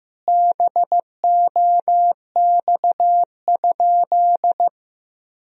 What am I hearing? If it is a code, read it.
Morse code, heard as a keyed tone, about 15 wpm, 700 Hz. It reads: BOX?
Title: Sentences from Top 1000 Words - Review Part 4 - 15wpm